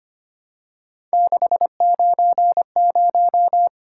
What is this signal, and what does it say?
Morse code, heard as a keyed tone, about 25 wpm, 700 Hz. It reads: 690
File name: Three Letter-Number Combinations - 25wpm